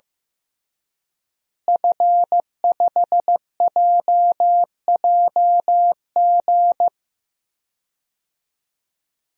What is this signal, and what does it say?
Morse code, heard as a keyed tone, about 15 wpm, 700 Hz. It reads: F5JJG
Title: DXCC-100 Call Signs - 5 Characters - Repeated 2x - 15wpm